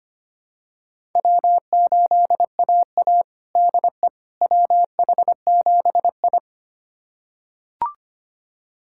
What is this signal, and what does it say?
Morse code, heard as a keyed tone, about 25 wpm, 700 Hz. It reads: W8AA DE WH7I
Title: USA Callsign DE Callsign - 4 Characters - 25wpm